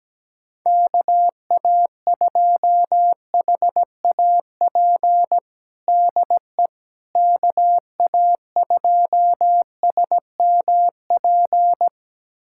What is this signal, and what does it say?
Morse code, heard as a keyed tone, about 17 wpm, 700 Hz. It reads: KA2HAP DE KA2SMP
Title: USA Callsign DE Callsign - 6 Characters - 17wpm